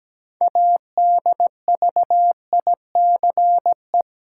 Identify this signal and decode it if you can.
Morse code, heard as a keyed tone, about 17 wpm, 700 Hz. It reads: ADVICE
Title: Six Letter Words - 17wpm